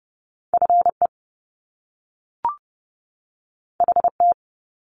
Morse code, heard as a keyed tone, about 30 wpm, 700 Hz. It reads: FE HT
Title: Top Two Letter Combinations in Words 30wpm